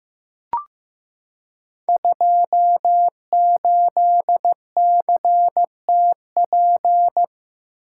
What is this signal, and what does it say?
Morse code, heard as a keyed tone, about 15 wpm, 700 Hz. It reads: 28CTP
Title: Code Groups - 5 Letter:Numbers - 15wpm